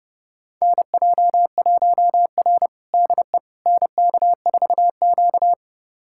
Morse code, heard as a keyed tone, about 32 wpm, 700 Hz. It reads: NJ1R DE NK4Q
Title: USA Callsign DE Callsign - 4 Characters - 30wpm